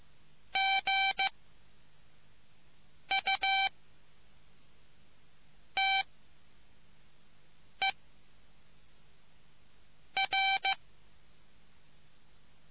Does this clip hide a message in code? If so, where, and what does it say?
Yes: Morse code from 0.6 to 10.7 s, heard as a keyed tone, about 15 wpm, 780 Hz. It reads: G U T E R